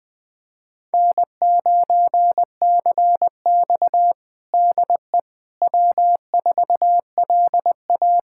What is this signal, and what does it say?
Morse code, heard as a keyed tone, about 21 wpm, 700 Hz. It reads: N9CX DE W4LA
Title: USA Callsign DE Callsign - 4 Characters - 20wpm